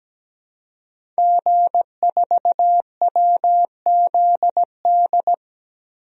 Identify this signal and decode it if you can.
Morse code, heard as a keyed tone, about 17 wpm, 700 Hz. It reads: G4WZD
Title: DXCC-100 Call Signs - 5 Characters - Repeated 2x - 17wpm